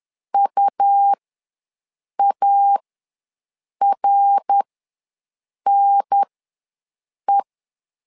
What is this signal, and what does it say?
Morse code, heard as a keyed tone, about 10 wpm, spaced as for 6 wpm, 800 Hz. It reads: UARNE
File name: LEZIONI / 3 MWRO / 3 MWRO ws 10wpm 2310